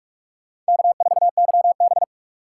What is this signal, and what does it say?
Morse code, heard as a keyed tone, about 47 wpm, 700 Hz. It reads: K4YB
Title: USA Call Signs - 4 Characters - 3x Character Spacing - 45wpm